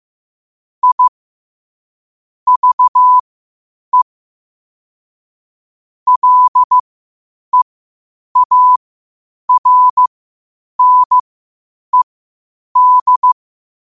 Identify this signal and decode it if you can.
Morse code, heard as a keyed tone, about 15 wpm, spaced as for 8 wpm, 1000 Hz. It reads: I VE LEARNED
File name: qotdfiles / 8wpm